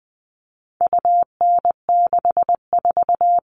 Morse code, heard as a keyed tone, about 20 wpm, 700 Hz. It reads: UN64
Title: All Cut Numbers - 4-Digit 20wpm